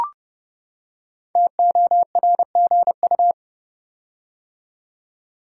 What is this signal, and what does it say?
Morse code, heard as a keyed tone, about 31 wpm, 700 Hz. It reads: TORGU
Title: Code Groups - 5 Letters - 30wpm